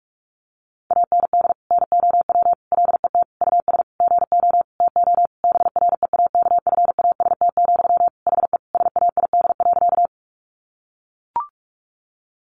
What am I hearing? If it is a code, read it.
Morse code, heard as a keyed tone, about 44 wpm, 700 Hz. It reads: AND NOW LET US GO TO BREAKFAST, HE SAID.
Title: Sentences from Top 1500 Words 45wpm